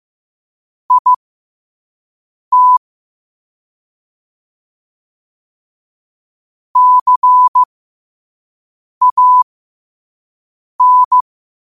Morse code, heard as a keyed tone, about 15 wpm, spaced as for 5 wpm, 1000 Hz. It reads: IT CAN
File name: qotdfiles / 5wpm